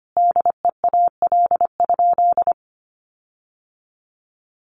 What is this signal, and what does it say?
Morse code, heard as a keyed tone, about 25 wpm, 700 Hz. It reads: DEAL?
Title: Sentences from Top 1400 Words 25wpm